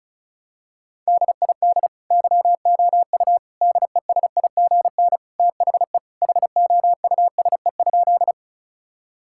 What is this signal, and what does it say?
Morse code, heard as a keyed tone, about 36 wpm, 700 Hz. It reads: DID YOU DESIGN THE HOUSE?
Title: Sentences from Top 1000 Words - Review Part 5 - 35wpm